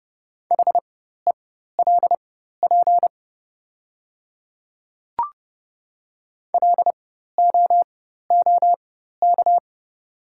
Morse code, heard as a keyed tone, about 31 wpm, spaced as for 14 wpm, 690 Hz. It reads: HELP LOOK
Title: 4x Character Spacing - Top 300 Words - 30wpm